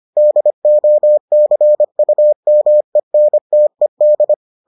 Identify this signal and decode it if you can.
Morse code, heard as a keyed tone, about 25 wpm, 600 Hz. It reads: DOCUMENTED